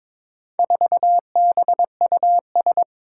Morse code, heard as a keyed tone, about 22 wpm, 700 Hz. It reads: 4BUS
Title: Four Challenging Letter-Number Combinations 22wpm